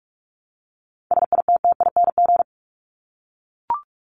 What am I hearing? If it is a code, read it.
Morse code, heard as a keyed tone, about 44 wpm, 700 Hz. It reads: SITTING
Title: ING Suffix - Top 100 Words with Base Words 45wpm